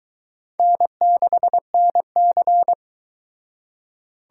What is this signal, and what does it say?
Morse code, heard as a keyed tone, about 23 wpm, 700 Hz. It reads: N6NC